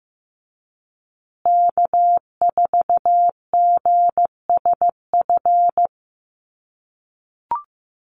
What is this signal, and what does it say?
Morse code, heard as a keyed tone, about 15 wpm, 700 Hz. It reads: K4GSF
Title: USA Call Signs Repeated 2x 15wpm